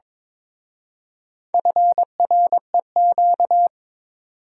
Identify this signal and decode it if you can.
Morse code, heard as a keyed tone, about 26 wpm, 700 Hz. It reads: FREQ
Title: Speed-Racing Abbreviations - 22wpm